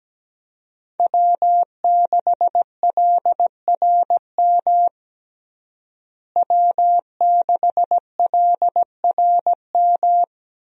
Morse code, heard as a keyed tone, about 17 wpm, 700 Hz. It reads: W6LRM W6LRM
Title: USA Call Signs - 5 Characters - Repeated 3x - 17wpm